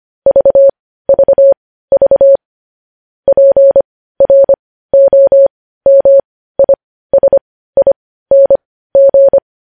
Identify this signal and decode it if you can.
Morse code, heard as a keyed tone, about 25 wpm, spaced as for 15 wpm, 550 Hz. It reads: VVV PROMISING
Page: news headlines in morse code at 15 wpm